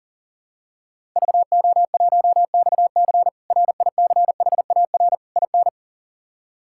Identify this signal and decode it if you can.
Morse code, heard as a keyed tone, about 40 wpm, 700 Hz. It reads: VO1XC RICHAR IN
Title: K1USN SST QSO Practice - 40wpm